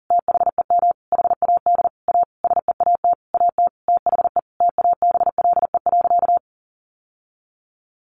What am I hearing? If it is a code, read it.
Morse code, heard as a keyed tone, about 40 wpm, 700 Hz. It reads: THEM HAD A SEAT AT THE TABLE.
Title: Sentences from Top 1500 Words 40wpm